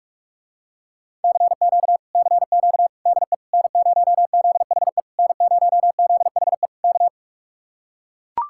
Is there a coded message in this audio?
Yes: Morse code from 1.2 to 7.1 s, heard as a keyed tone, about 44 wpm, 700 Hz. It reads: CQ CQ DE N0ZHE N0ZHE K